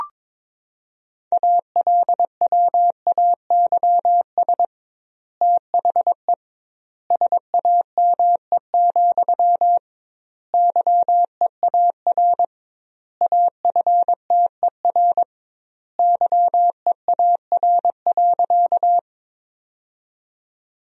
Morse code, heard as a keyed tone, about 22 wpm, 700 Hz. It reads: ALWAYS THE SAME, YEAR AFTER YEAR.
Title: 2x Word-Spacing - Sentences from Top 300 Words - 22wpm